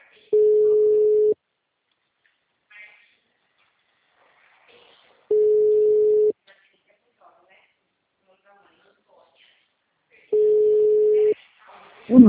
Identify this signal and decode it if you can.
Ringback tone (Europe, 425 Hz), with speech.
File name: out- -2013- - - .5589